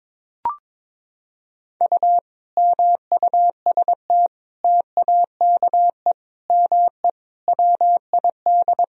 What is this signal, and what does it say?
Morse code, heard as a keyed tone, about 22 wpm, 700 Hz. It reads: U MUST TAKE ME WID
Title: Sentences from Top 200 Words and Abbreviations 22wpm